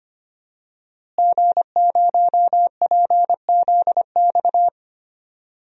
Morse code, heard as a keyed tone, about 25 wpm, 700 Hz. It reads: G0PZX
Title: DXCC-100 Call Signs - 5 Characters - Repeated 3x - 25wpm